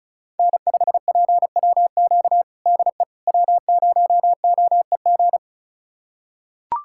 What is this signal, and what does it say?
Morse code, heard as a keyed tone, about 36 wpm, 700 Hz. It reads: N5PWQ DE W0OEG
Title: USA Callsign DE Callsign - 5 Characters - 35wpm